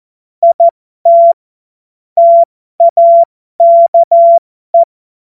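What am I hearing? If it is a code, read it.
Morse code, heard as a keyed tone, about 14 wpm, spaced as for 12 wpm, 680 Hz. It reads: IT TAKE
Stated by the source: Morse code joke practice.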